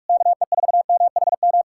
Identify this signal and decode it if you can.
Morse code, heard as a keyed tone, about 59 wpm, spaced as for 52 wpm, 700 Hz. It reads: KE4MHM
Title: Speed-Racing Common QSO Phrases - 45wpm